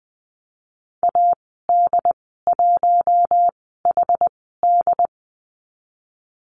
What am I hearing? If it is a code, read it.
Morse code, heard as a keyed tone, about 20 wpm, spaced as for 15 wpm, 700 Hz. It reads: AD1HD
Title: 128 Lesson 21g - Number 6 - Call Signs 20wpm